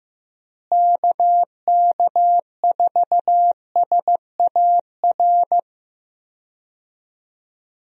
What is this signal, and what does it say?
Morse code, heard as a keyed tone, about 15 wpm, 700 Hz. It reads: KK4SAR
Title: USA Call Signs - 6 Characters - Repeated 3x - 15wpm